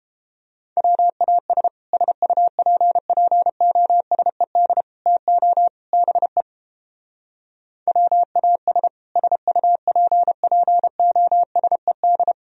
Morse code, heard as a keyed tone, about 29 wpm, 700 Hz. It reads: WAS SUPPOSED TO BE WAS SUPPOSED